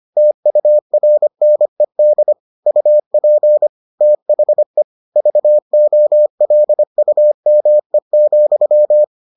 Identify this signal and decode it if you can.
Morse code, heard as a keyed tone, about 25 wpm, 600 Hz. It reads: TURNED UP THE VOLUME,